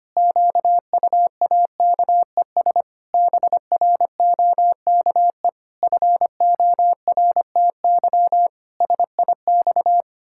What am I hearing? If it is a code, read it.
Morse code, heard as a keyed tone, about 25 wpm, 700 Hz. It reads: QUAKES BROKE FORTY SIX
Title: Pangram Challenge v2 - 25wpm